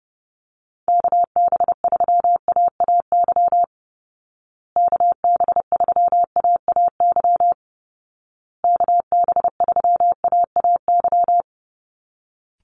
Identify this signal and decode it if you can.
Morse code, heard as a keyed tone, about 30 wpm, 700 Hz. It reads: KB3AAY KB3AAY KB3AAY
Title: Call Signs Repeated 2x - 30wpm